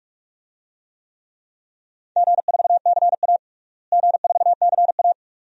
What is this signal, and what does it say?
Morse code, heard as a keyed tone, about 43 wpm, 700 Hz. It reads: G4CA G4CA
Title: DXCC-100 Call Signs - 4 Characters - Repeated 2x - 45wpm